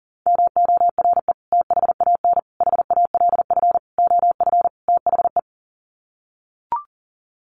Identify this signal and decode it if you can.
Morse code, heard as a keyed tone, about 40 wpm, 700 Hz. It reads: MORE THAN HALF OF THE
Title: Sets of 5 Words 40wpm